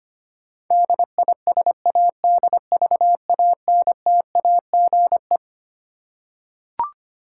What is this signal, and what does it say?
Morse code, heard as a keyed tone, about 25 wpm, 700 Hz. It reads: DISADVANTAGE
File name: Twelve Letter Words - 25wpm